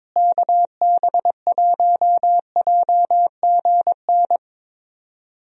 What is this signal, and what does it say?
Morse code, heard as a keyed tone, about 22 wpm, 700 Hz. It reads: KB1JGN
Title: USA Call Signs - 6 Characters - Repeated 3x - 22wpm